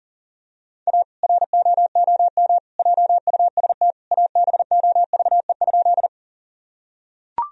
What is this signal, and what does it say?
Morse code, heard as a keyed tone, about 40 wpm, 700 Hz. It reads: A ROOM JUST ABOVE?
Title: Sentences from Top 1000 Words - Review Part 2 - 40wpm